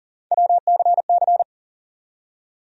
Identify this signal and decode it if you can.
Morse code, heard as a keyed tone, about 40 wpm, 700 Hz. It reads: WCC
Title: USA Call Sign Postfixes - 3 Letters 40wpm